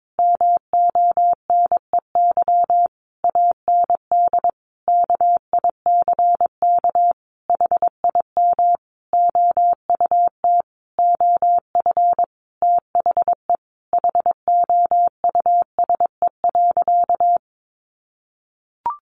Morse code, heard as a keyed tone, about 22 wpm, 700 Hz. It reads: MONEY AND KICK HIM OUT OF THE HOUSE.
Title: Sentences from Top 1300 Words 22wpm